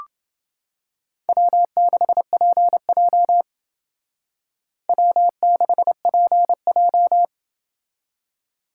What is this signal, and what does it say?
Morse code, heard as a keyed tone, about 28 wpm, 700 Hz. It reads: W6PJ W6PJ